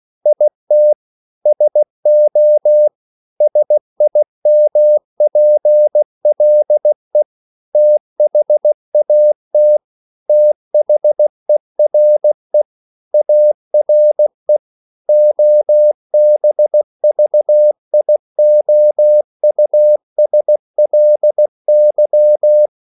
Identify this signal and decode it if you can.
Morse code, heard as a keyed tone, about 16 wpm, 600 Hz. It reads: IT SO SIMPLE THAT THERE ARE OBVIOUSLY
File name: CW / Practice / EnglishQuotations0216